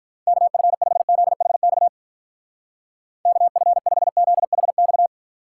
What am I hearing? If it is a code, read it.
Morse code, heard as a keyed tone, about 42 wpm, 700 Hz. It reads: KF5ZHX KF5ZHX
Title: Speed Racing - USA 2x3 Call Signs 35wpm